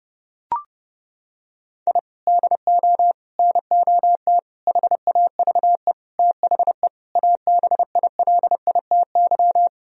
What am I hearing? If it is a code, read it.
Morse code, heard as a keyed tone, about 32 wpm, 700 Hz. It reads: I DO NOT HAVE THE ABILITY
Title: Sentences from Top 1000 Words - Review Part 3 - 30wpm